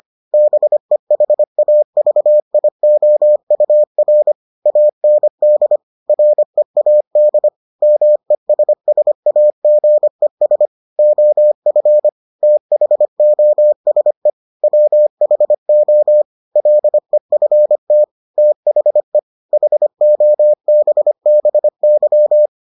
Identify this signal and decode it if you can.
Morse code, heard as a keyed tone, about 25 wpm, 600 Hz. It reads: BEHAVIOUR AND READ MESSAGES OF THOSE WHO LEFT THE HOBBY